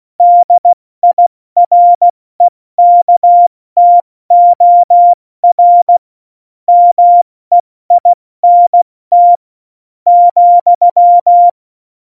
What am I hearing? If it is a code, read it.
Morse code, heard as a keyed tone, about 16 wpm, 700 Hz. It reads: DIREKTOR MEINT ,
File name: Morsekurs / Witze / Witz 02. Oktober 2018, 70 F80